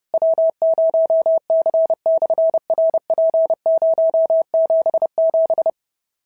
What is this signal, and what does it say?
Morse code, heard as a keyed tone, about 30 wpm, 650 Hz. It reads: W0C/RP077
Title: SOTA QSO Practice 30wpm